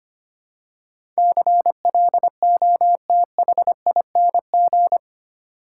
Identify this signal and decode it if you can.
Morse code, heard as a keyed tone, about 25 wpm, 700 Hz. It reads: CLOTHING
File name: 8 Letter Words - First 3 Letters Given - 25wpm